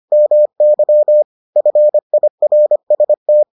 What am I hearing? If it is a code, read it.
Morse code, heard as a keyed tone, about 25 wpm, 600 Hz. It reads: MY FIRST